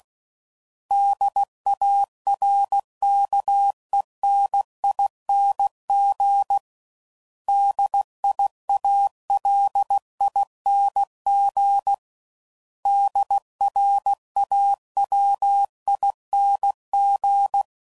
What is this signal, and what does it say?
Morse code, heard as a keyed tone, about 16 wpm, 800 Hz. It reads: DARKENING DIALING DRAWING
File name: Suffixes / Special INGs / Sngl Ings